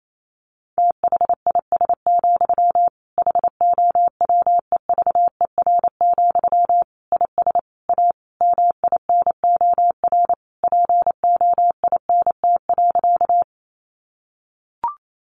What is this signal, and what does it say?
Morse code, heard as a keyed tone, about 28 wpm, 700 Hz. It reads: THIS, HOWEVER, IS A MINOR POINT.
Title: Sentences from Top 1500 Words 28wpm